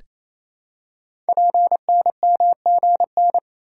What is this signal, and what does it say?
Morse code, heard as a keyed tone, about 29 wpm, 700 Hz. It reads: PNMGN
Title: Code Groups - 5 Letters - 28wpm